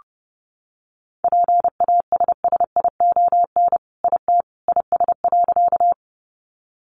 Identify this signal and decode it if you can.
Morse code, heard as a keyed tone, about 30 wpm, 700 Hz. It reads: PASSION IT IS.
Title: Sentences from Top 1300 Words 30wpm